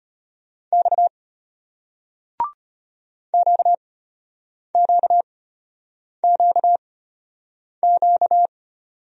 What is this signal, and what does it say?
Morse code, heard as a keyed tone, about 34 wpm, spaced as for 8 wpm, 700 Hz. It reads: X QQQQ